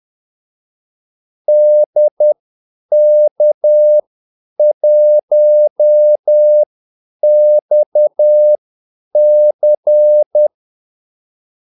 Morse code, heard as a keyed tone, about 10 wpm, spaced as for 8 wpm, 600 Hz. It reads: DK1XC
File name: Callsigns-10-8wpm